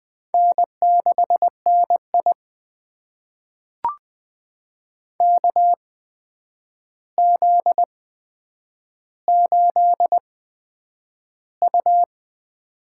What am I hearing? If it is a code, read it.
Morse code, heard as a keyed tone, about 20 wpm, spaced as for 9 wpm, 700 Hz. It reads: N6NI KZ8U
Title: USA Call Signs - 4 Characters - 8x Character Spacing - 20wpm